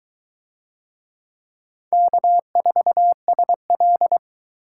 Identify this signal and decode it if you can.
Morse code, heard as a keyed tone, about 22 wpm, 700 Hz. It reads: K4SL